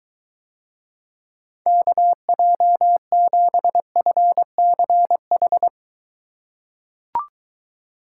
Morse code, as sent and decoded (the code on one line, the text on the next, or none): -.- .--- --... ..-. -.-. ....
KJ7FCH